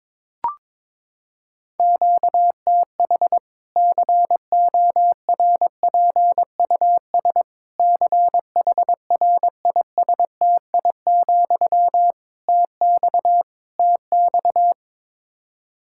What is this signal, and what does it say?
Morse code, heard as a keyed tone, about 22 wpm, 700 Hz. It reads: QTH CORPUS CHRISTI, TX TX
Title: Practice QSO - Exchange 1 - Beginning - 22wpm